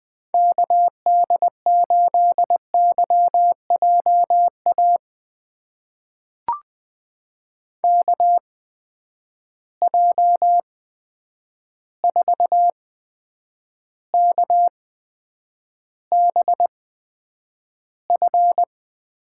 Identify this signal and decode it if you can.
Morse code, heard as a keyed tone, about 20 wpm, spaced as for 9 wpm, 700 Hz. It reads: KD8YJA KJ4KBF
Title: USA Call Signs - 8x Character Spacing - 20wpm